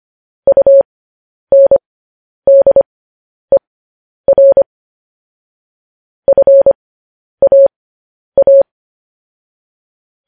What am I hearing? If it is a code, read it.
Morse code, heard as a keyed tone, about 25 wpm, spaced as for 10 wpm, 550 Hz. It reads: UNDER FAA